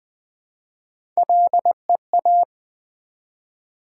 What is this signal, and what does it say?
Morse code, heard as a keyed tone, about 20 wpm, 700 Hz. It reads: LEA
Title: Build A Word - 1 Letter At A Time - 5-Letter Words - 20wpm